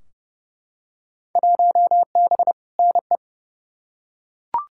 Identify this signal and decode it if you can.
Morse code, heard as a keyed tone, about 31 wpm, 700 Hz. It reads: 1B NE
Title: Field Day Exchanges Encore - 30wpm